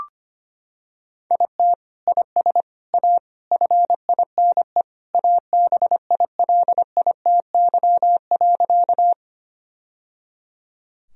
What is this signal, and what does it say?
Morse code, heard as a keyed tone, about 25 wpm, 700 Hz. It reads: IT IS A FINE ABILITY.
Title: Sentences from Top 1000 Words - Review Part 3 - 25wpm